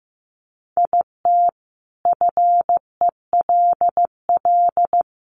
Morse code, heard as a keyed tone, about 14 wpm, 700 Hz. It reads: IT FELL
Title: A Sound Of Thunder 15wpm